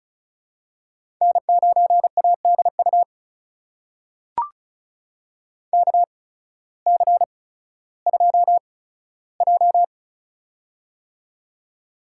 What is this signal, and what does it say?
Morse code, heard as a keyed tone, about 36 wpm, spaced as for 16 wpm, 700 Hz. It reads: N9ADU KC2J
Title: USA Call Signs - 8x Character Spacing - 35wpm